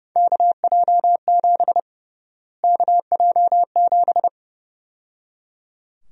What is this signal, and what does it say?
Morse code, heard as a keyed tone, about 31 wpm, 700 Hz. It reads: KJ7 KJ7
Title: USA Call Sign Prefixes - Repeated 2x - 30wpm